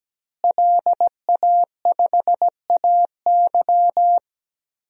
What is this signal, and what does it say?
Morse code, heard as a keyed tone, about 17 wpm, 700 Hz. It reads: LA5AY
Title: DXCC-100 Call Signs - 5 Characters - Repeated 2x - 17wpm